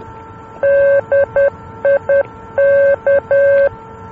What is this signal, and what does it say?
Morse code, heard as a keyed tone, about 10 wpm, 560 Hz. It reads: DIK
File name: luxembourg / mx / DIK-L-NDB